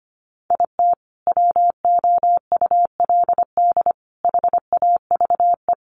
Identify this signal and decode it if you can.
Morse code, heard as a keyed tone, about 25 wpm, 700 Hz. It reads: IT WOULD HAVE
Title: Sets of 5 Words Encore 25wpm